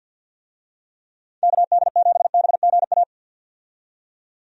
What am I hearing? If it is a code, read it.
Morse code, heard as a keyed tone, about 47 wpm, 700 Hz. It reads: KD7BGA
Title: USA Call Signs - 6 Characters - 50wpm